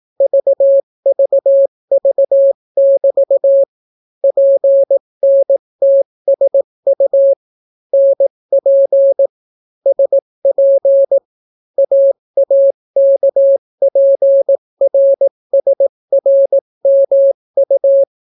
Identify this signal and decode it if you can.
Morse code, heard as a keyed tone, about 18 wpm, 550 Hz. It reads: VVV<BT> PNTSU NP SP AAKPRSRMU